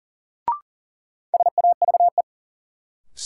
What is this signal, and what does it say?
Morse code, heard as a keyed tone, about 40 wpm, 700 Hz. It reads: SAVE
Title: Sentences from Top 1100 Words - New Word Drill - 40wpm